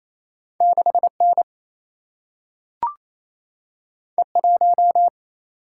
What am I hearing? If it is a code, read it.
Morse code, heard as a keyed tone, about 28 wpm, 700 Hz. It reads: 6N E1
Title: Single-Letter Single-Number 28wpm